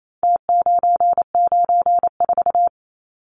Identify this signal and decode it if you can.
Morse code, heard as a keyed tone, about 28 wpm, 700 Hz. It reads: T994
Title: Numbers - Contest Style - 4-Digit 28wpm